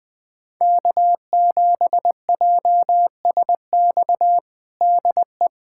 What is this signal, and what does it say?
Morse code, heard as a keyed tone, about 20 wpm, 700 Hz. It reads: K7JSX DE
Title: USA Callsign DE Callsign - 5 Characters - 20wpm